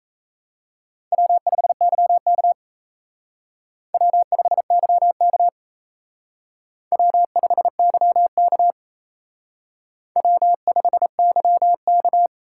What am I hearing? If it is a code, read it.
Morse code, heard as a keyed tone, about 35 wpm, 700 Hz. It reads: W5YK W5YK W5YK W5YK
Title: Speed Racing - USA 1x2 Call Signs 28wpm